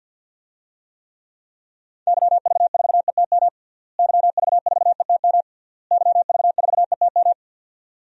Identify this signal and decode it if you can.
Morse code, heard as a keyed tone, about 52 wpm, 700 Hz. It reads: YV4ETM YV4ETM YV4ETM
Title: DXCC-100 Call Signs - 6 Characters - Repeated 3x - 50wpm